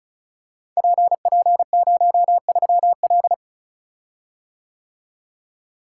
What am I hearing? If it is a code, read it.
Morse code, heard as a keyed tone, about 36 wpm, 700 Hz. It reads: PP03L
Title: Code Groups - 5 Letter:Numbers - 35wpm